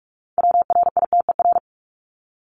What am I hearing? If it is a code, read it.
Morse code, heard as a keyed tone, about 44 wpm, 700 Hz. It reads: WRITER
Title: Top 1751-2000 Words 45wpm